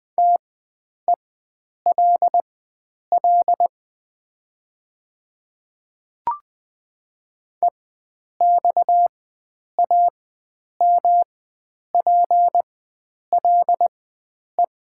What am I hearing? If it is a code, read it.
Morse code, heard as a keyed tone, about 20 wpm, spaced as for 9 wpm, 700 Hz. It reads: TELL EXAMPLE
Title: 4x Character Spacing - Top 300 Words - 20wpm